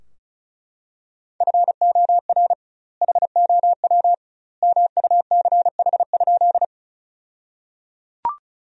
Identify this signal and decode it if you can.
Morse code, heard as a keyed tone, about 34 wpm, 700 Hz. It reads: FOR HOW MUCH?
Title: 2x Word-Spacing - Sentences from Top 200 Words - 35wpm